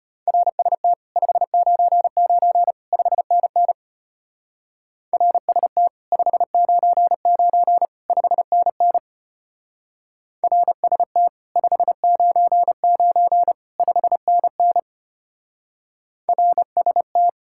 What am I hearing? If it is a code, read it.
Morse code, heard as a keyed tone, about 29 wpm, 700 Hz. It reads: RST 599 5NN RST 599 5NN RST 599 5NN RST
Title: Speed-Racing Common QSO Phrases Encore - 25wpm